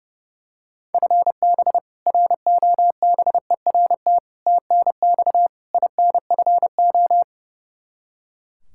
Morse code, heard as a keyed tone, about 32 wpm, 700 Hz. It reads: FB ROBERT TNX INFO
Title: K1USN SST QSO Practice 30wpm